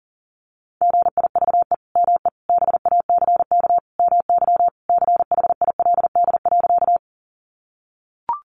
Morse code, heard as a keyed tone, about 40 wpm, 700 Hz. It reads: GIVE ME BACK MY CHILD.
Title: Sentences from Top 800 Words 40wpm